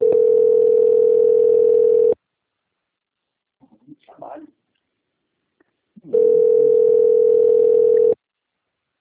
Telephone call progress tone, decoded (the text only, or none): ringback tone